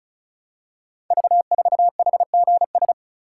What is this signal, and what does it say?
Morse code, heard as a keyed tone, about 36 wpm, 700 Hz. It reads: V4HGS
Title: Code Groups - 5 Letter:Numbers - 35wpm